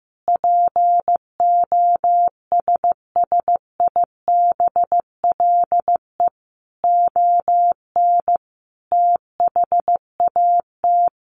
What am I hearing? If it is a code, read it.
Morse code, heard as a keyed tone, about 15 wpm, 700 Hz. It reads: POSSIBLE ON THAT
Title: Sentences from Top 1400 Words 15wpm